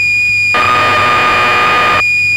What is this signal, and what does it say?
Cassette data (Kansas City type).
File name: magazines / Acorn User / 1984-12 BBC Programs